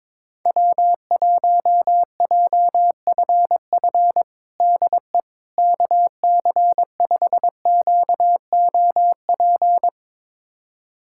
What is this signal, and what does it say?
Morse code, heard as a keyed tone, about 22 wpm, 700 Hz. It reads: W1JFF DE KC5QOP
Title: USA Callsign DE Callsign 22wpm